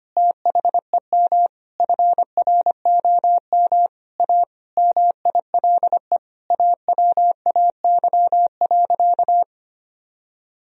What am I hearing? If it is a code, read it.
Morse code, heard as a keyed tone, about 25 wpm, 700 Hz. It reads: THEM FROM A MILE AWAY.
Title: Sentences from Top 1000 Words - Review Part 3 - 25wpm